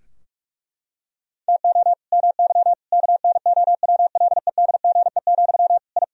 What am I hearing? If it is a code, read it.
Morse code, heard as a keyed tone, about 41 wpm, 700 Hz. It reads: TO MY KNOWLEDGE, I
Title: Sentences from Top 1000 Words - Review Part 3 - 45wpm